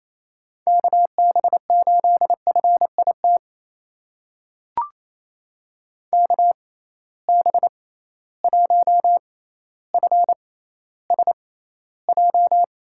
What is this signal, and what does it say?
Morse code, heard as a keyed tone, about 28 wpm, spaced as for 14 wpm, 700 Hz. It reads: KB8FIT KB1FSJ
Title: USA Call Signs - 6 Characters - 6x Character Spacing - 28wpm